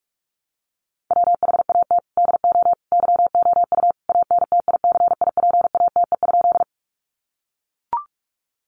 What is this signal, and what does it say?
Morse code, heard as a keyed tone, about 44 wpm, 700 Hz. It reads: WHAT DO YOU ANTICIPATE?
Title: Sentences from Top 1500 Words 45wpm